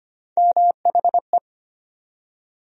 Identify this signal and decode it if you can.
Morse code, heard as a keyed tone, about 25 wpm, 700 Hz. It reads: MHE
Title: USA Call Sign Postfixes - 3 Letters 25wpm